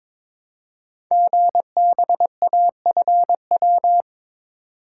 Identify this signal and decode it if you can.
Morse code, heard as a keyed tone, about 23 wpm, 700 Hz. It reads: GBAFW
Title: Code Groups - 5 Letters - 22wpm